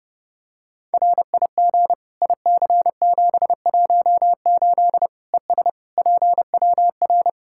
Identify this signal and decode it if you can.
Morse code, heard as a keyed tone, about 30 wpm, 700 Hz. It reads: RIG IC718 ES PWR
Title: Practice QSO - Exchange 2 - Info - 30wpm